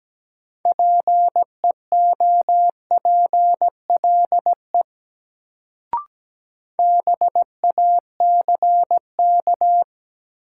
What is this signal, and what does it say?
Morse code, heard as a keyed tone, about 17 wpm, 700 Hz. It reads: PEOPLE BACK
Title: Sets of 2 Words - New Word Drill - 17wpm